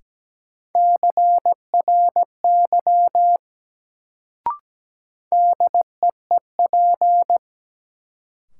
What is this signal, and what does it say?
Morse code, heard as a keyed tone, about 17 wpm, 700 Hz. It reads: CRY DEEP
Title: Sets of 5 Words Encore - New Word Drill - 17wpm